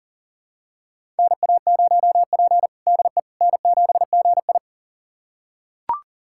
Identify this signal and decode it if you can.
Morse code, heard as a keyed tone, about 40 wpm, 700 Hz. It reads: NA0P DE N7GI
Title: USA Callsign DE Callsign - 4 Characters - 40wpm